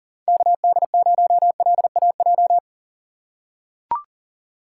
Morse code, heard as a keyed tone, about 40 wpm, 700 Hz. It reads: KD0LAJ
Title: USA Call Signs - 6 Characters - 2x Character Spacing - 40wpm